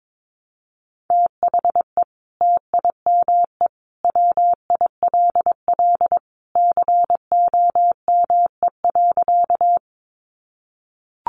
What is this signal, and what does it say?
Morse code, heard as a keyed tone, about 22 wpm, 700 Hz. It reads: THE TIME WILL COME.
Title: Sentences from Top 100 Words 22wpm